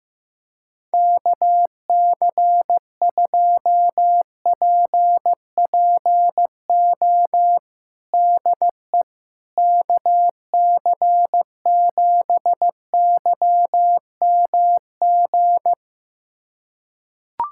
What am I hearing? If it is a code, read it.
Morse code, heard as a keyed tone, about 15 wpm, 700 Hz. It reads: KC2PPO DE KC7YMG
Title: USA Callsign DE Callsign - 6 Characters - 15wpm